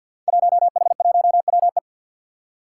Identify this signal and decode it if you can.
Morse code, heard as a keyed tone, about 49 wpm, 700 Hz. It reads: 1H1WE
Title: Five Challenging Letter-Number Combinations 50wpm